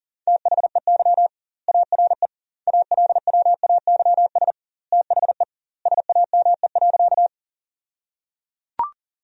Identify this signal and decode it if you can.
Morse code, heard as a keyed tone, about 37 wpm, 700 Hz. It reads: THEY ARE ALWAYS THE SAME.
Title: 2x Word-Spacing - Sentences from Top 300 Words - 40wpm